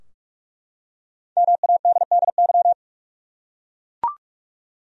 Morse code, heard as a keyed tone, about 44 wpm, 700 Hz. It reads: MADDY
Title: USA Common Names - Max 5 Letters - 45wpm